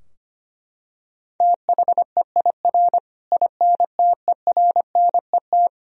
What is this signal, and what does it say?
Morse code, heard as a keyed tone, about 25 wpm, 700 Hz. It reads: THEIR INTERNET
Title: Sentences from Top 1000 Words - Review Part 4 - 25wpm